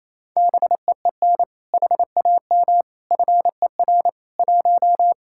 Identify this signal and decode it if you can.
Morse code, heard as a keyed tone, about 35 wpm, 700 Hz. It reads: BEEN HAM FER 1
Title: Speed-Racing Common QSO Phrases - 28wpm